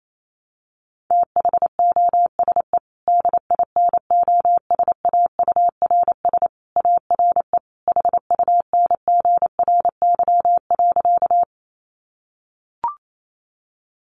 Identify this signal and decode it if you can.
Morse code, heard as a keyed tone, about 28 wpm, 700 Hz. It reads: THOSE DINOSAURS ARE HUNGRY.
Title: A Sound Of Thunder 28wpm